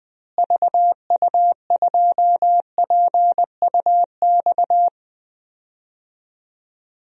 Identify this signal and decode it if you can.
Morse code, heard as a keyed tone, about 20 wpm, 700 Hz. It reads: VU2PUX
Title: DXCC-100 Call Signs - 6 Characters - Repeated 3x - 20wpm